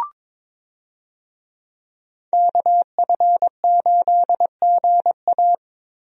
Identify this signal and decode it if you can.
Morse code, heard as a keyed tone, about 22 wpm, 700 Hz. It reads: KF8GA
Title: USA Call Signs - 5 Characters - Repeated 2x - 22wpm